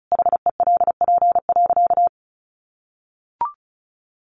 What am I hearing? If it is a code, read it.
Morse code, heard as a keyed tone, about 34 wpm, 700 Hz. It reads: HELP.
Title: Sentences from Top 700 Words 35wpm